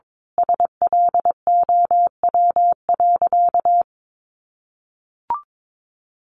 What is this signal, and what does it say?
Morse code, heard as a keyed tone, about 22 wpm, 700 Hz. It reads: SLOW.
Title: Sentences from Top 1400 Words 22wpm